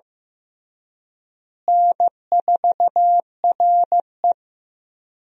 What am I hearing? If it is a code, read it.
Morse code, heard as a keyed tone, about 16 wpm, 700 Hz. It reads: N4RE
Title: USA Call Signs - 4 Characters - Repeated 3x - 15wpm